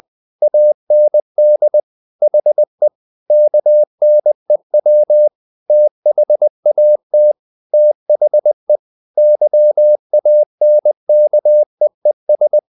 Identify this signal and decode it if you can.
Morse code, heard as a keyed tone, about 20 wpm, 600 Hz. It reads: AND HE KNEW THAT THE YANKEES
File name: the old man and the sea / the old man and the sea part 1 20wpm